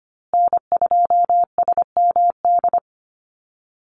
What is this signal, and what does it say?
Morse code, heard as a keyed tone, about 25 wpm, 700 Hz. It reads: N2SMD
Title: USA Call Signs Repeated 2x 25wpm